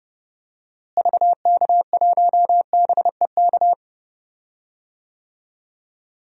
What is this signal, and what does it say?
Morse code, heard as a keyed tone, about 32 wpm, 700 Hz. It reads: VK1BEK
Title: DXCC-100 Call Signs - 6 Characters - Repeated 3x - 30wpm